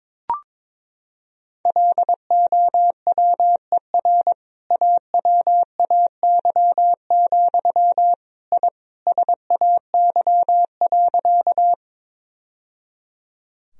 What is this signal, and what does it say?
Morse code, heard as a keyed tone, about 22 wpm, 700 Hz. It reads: LOWER AWAY, I SAY.
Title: Sentences from Top 1000 Words - Review Part 4 - 22wpm